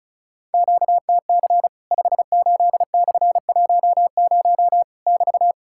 Morse code, heard as a keyed tone, about 36 wpm, 700 Hz. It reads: QTC 58/10 <BT>
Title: Worked All Europe DX Contest - QTC Exchange - 35wpm